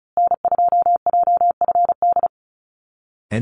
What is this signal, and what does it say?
Morse code, heard as a keyed tone, about 34 wpm, 700 Hz. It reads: N2JFD
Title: Call Signs Repeated 2x - 35wpm